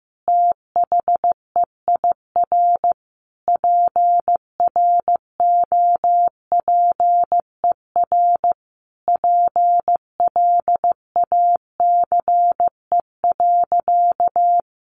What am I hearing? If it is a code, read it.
Morse code, heard as a keyed tone, about 15 wpm, 700 Hz. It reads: THEIR PROPER PLACE.
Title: Sentences from Top 800 Words 15wpm